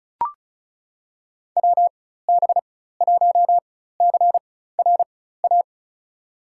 Morse code, heard as a keyed tone, about 36 wpm, spaced as for 16 wpm, 700 Hz. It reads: WB1CRA
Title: USA Call Signs - 4x Character Spacing - 35wpm